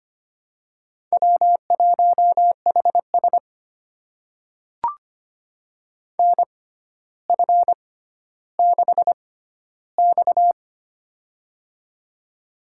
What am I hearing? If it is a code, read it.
Morse code, heard as a keyed tone, about 25 wpm, spaced as for 13 wpm, 700 Hz. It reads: W1HS NF6X
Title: USA Call Signs - 4 Characters - 6x Character Spacing - 25wpm